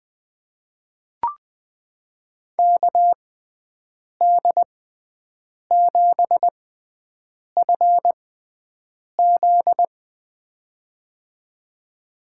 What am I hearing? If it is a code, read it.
Morse code, heard as a keyed tone, about 20 wpm, spaced as for 7 wpm, 700 Hz. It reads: KD7FZ
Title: USA Call Signs - 5 Characters - 6x Character Spacing - 20wpm